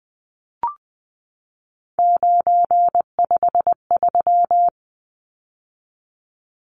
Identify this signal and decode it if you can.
Morse code, heard as a keyed tone, about 20 wpm, 700 Hz. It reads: 953
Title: Numbers - 3-Digit 20wpm